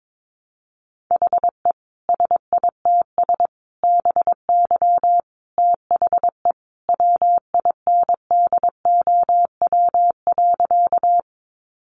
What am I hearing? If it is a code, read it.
Morse code, heard as a keyed tone, about 22 wpm, 700 Hz. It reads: HE SITS BY THE WINDOW.
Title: Easy Phrases 22wpm